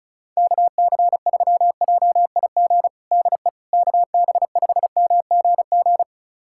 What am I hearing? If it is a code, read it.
Morse code, heard as a keyed tone, about 36 wpm, 700 Hz. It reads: KC3JIG DE KB5MGG
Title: USA Callsign DE Callsign 35wpm